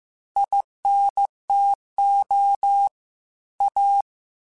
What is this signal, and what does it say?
Morse code, heard as a keyed tone, about 15 wpm, 800 Hz. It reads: INTO A
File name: QSOs / QSOs-mp3s / kc7sac@13